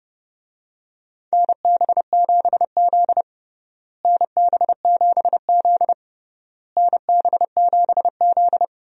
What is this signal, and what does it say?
Morse code, heard as a keyed tone, about 32 wpm, 700 Hz. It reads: NB7Z NB7Z NB7Z
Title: USA Call Signs - 4 Characters - Repeated 3x - 30wpm